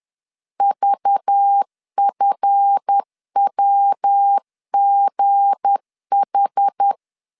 Morse code, heard as a keyed tone, about 10 wpm, 800 Hz. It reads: VFWGH
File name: LEZIONI / 8 YJ45 / 8 YJ 10wpm 2651